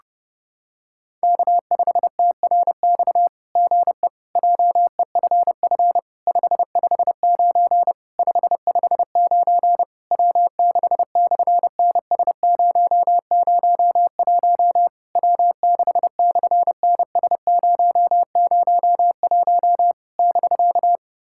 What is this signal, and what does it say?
Morse code, heard as a keyed tone, about 30 wpm, 700 Hz. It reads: K5TRX GE JEFF 559 559 W6/NS001 W6/NS001 <BK>
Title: SOTA QSO Practice 30wpm